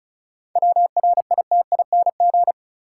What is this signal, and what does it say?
Morse code, heard as a keyed tone, about 36 wpm, 700 Hz. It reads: WRITING
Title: Sentences from Top 800 Words - New Word Drill - 35wpm